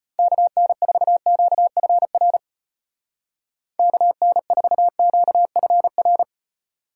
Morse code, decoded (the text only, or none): KN4QFR KN4QFR